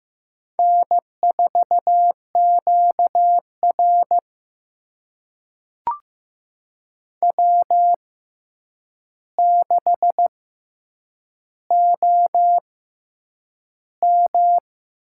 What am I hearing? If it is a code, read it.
Morse code, heard as a keyed tone, about 15 wpm, spaced as for 8 wpm, 700 Hz. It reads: N4QR W6OM
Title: USA Call Signs - 4 Characters - 6x Character Spacing - 15wpm